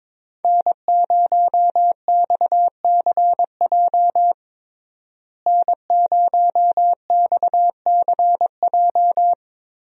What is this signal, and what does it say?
Morse code, heard as a keyed tone, about 22 wpm, 700 Hz. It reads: N0XCJ N0XCJ
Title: USA Call Signs - 5 Characters - Repeated 3x - 22wpm